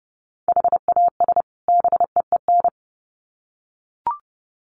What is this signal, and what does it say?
Morse code, heard as a keyed tone, about 30 wpm, 700 Hz. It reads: HAS BEEN
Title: Sets of 2 Words 30wpm